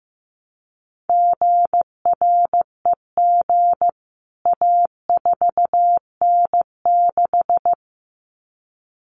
Morse code, heard as a keyed tone, about 14 wpm, 700 Hz. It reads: GREG A4N6
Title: CWT Exchange 15wpm